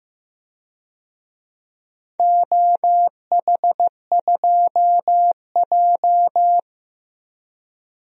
Morse code, heard as a keyed tone, about 15 wpm, 700 Hz. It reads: OH2J
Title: DXCC-100 Call Signs - 4 Characters - Repeated 3x - 15wpm